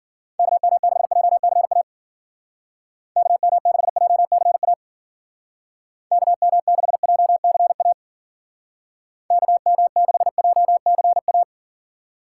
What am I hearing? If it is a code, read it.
Morse code, heard as a keyed tone, about 49 wpm, 700 Hz. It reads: KM6JCA KM6JCA KM6JCA KM6JCA
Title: Speed Racing - USA Call Signs 40wpm